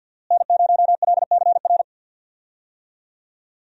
Morse code, heard as a keyed tone, about 47 wpm, 700 Hz. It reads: N0LCR
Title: USA Call Signs - 5 Characters - Repeated 2x - 50wpm